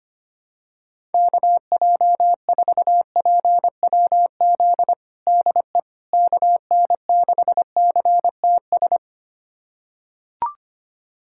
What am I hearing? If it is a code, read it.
Morse code, heard as a keyed tone, about 25 wpm, 700 Hz. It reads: KJ4PWZ DE KN6CTS
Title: USA Callsign DE Callsign 25wpm